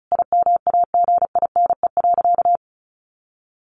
Morse code, heard as a keyed tone, about 34 wpm, 700 Hz. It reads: IMAGINE.
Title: Sentences from Top 1400 Words 35wpm